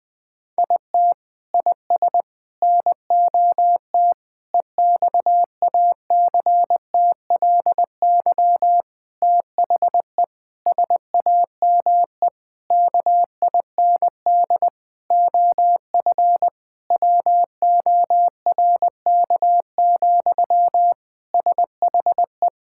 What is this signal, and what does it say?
Morse code, heard as a keyed tone, about 21 wpm, 700 Hz. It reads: IT IS NOT EXACTLY THE SAME KIND OF WORK, SHE
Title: Sentences from Top 1000 Words - Review Part 6 - 20wpm